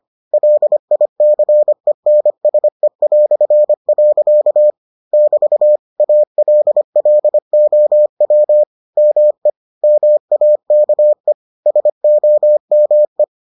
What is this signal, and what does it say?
Morse code, heard as a keyed tone, about 25 wpm, 600 Hz. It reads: LICENSE". <BT> ALLOW ME MAKE SOME